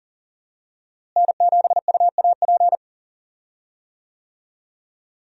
Morse code, heard as a keyed tone, about 40 wpm, 700 Hz. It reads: N7UAP
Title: USA Call Signs - 5 Characters - Repeated 3x - 40wpm